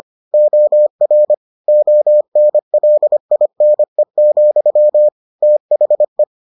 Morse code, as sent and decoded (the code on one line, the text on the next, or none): --- .-. / --- -. .-.. .. -. --..-- / - .... .
OR ONLINE, THE